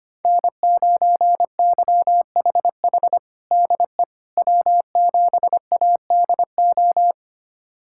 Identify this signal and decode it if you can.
Morse code, heard as a keyed tone, about 25 wpm, 700 Hz. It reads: N9YHH DE W7ADO
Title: USA Callsign DE Callsign - 5 Characters - 25wpm